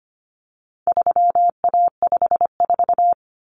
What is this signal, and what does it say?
Morse code, heard as a keyed tone, about 25 wpm, 700 Hz. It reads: 3A54
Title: Common Cut Numbers - 4-Digit 25wpm